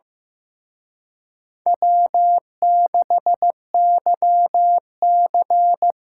Morse code, heard as a keyed tone, about 18 wpm, 700 Hz. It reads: W6YC
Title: Speed Racing - USA 1x2 Call Signs 15wpm